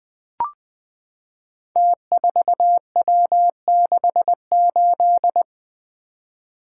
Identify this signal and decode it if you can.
Morse code, heard as a keyed tone, about 20 wpm, 700 Hz. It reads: T4W68
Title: Code Groups - 5 Letter:Numbers - 20wpm